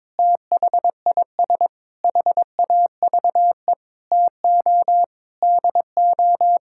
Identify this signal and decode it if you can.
Morse code, heard as a keyed tone, about 22 wpm, 700 Hz. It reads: THIS HAVE TO DO
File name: Sets of 9 Words - 22wpm